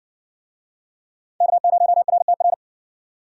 Morse code, heard as a keyed tone, about 57 wpm, 700 Hz. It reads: K0RTR